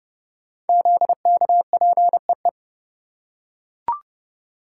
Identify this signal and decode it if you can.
Morse code, heard as a keyed tone, about 30 wpm, 700 Hz. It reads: ZKPEE
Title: Code Groups - 5 Letters - 30wpm